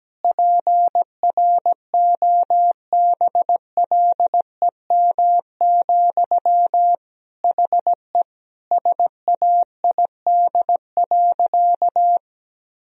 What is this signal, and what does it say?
Morse code, heard as a keyed tone, about 17 wpm, 700 Hz. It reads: PROBLEM, HE SAID.
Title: Sentences from Top 1000 Words - Review Part 2 - 17wpm